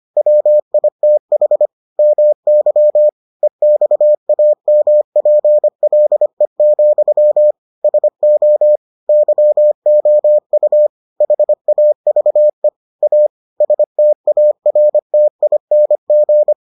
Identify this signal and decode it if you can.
Morse code, heard as a keyed tone, about 25 wpm, 600 Hz. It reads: WITH MY EXAMPLE, SO YOU HAVE A STARTING